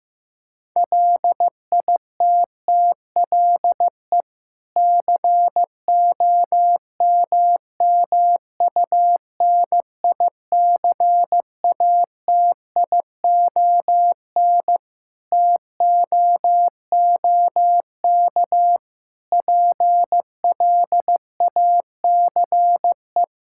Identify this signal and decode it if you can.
Morse code, heard as a keyed tone, about 16 wpm, 700 Hz. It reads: LITTLE COMMUNICATION TOOK PLACE
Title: Sentences from Top 1000 Words - Review Part 6 - 15wpm